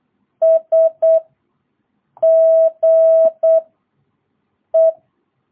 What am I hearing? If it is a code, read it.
Morse code, heard as a keyed tone, about 8 wpm, 660 Hz. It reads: S G E
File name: jh8cbh / 2022cw